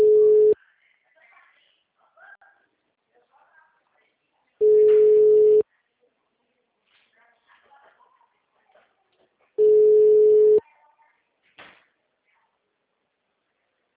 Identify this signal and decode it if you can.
Ringback tone (Europe, 425 Hz), with speech.